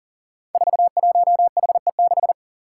Morse code, heard as a keyed tone, about 37 wpm, 700 Hz. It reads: 41HE6
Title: Five Challenging Letter-Number Combinations 40wpm